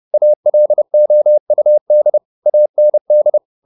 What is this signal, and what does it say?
Morse code, heard as a keyed tone, about 30 wpm, 600 Hz. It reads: ALOUD AND
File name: the old man and the sea / the old man and the sea 1 30wpm